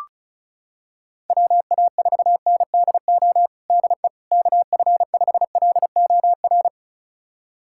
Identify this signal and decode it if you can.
Morse code, heard as a keyed tone, about 36 wpm, 700 Hz. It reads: WA4NDO DE KF5LOR
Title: USA Callsign DE Callsign 35wpm